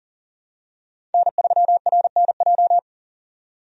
Morse code, heard as a keyed tone, about 37 wpm, 700 Hz. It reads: N3RNJ
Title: USA Call Signs - 5 Characters - 40wpm